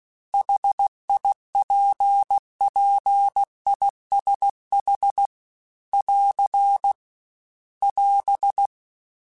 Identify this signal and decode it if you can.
Morse code, heard as a keyed tone, about 16 wpm, 800 Hz. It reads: HIPPISH <AR> <AS>